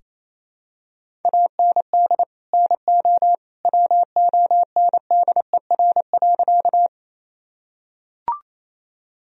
Morse code, heard as a keyed tone, about 28 wpm, 700 Hz. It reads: AND NO WONDER.
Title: Sentences from Top 1000 Words - Review Part 3 - 28wpm